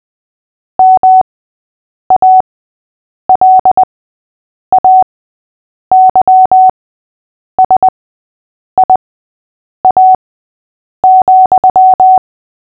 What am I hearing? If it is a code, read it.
Morse code, heard as a keyed tone, about 20 wpm, spaced as for 8 wpm, 750 Hz. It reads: MALAYSIA,
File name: cw / fists / Short-Story-3-8wpm